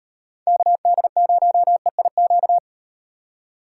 Morse code, heard as a keyed tone, about 35 wpm, 700 Hz. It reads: KD0EIQ